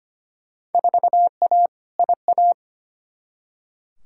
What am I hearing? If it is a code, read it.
Morse code, heard as a keyed tone, about 25 wpm, 700 Hz. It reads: 4A IA
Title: Field Day Exchanges - 25wpm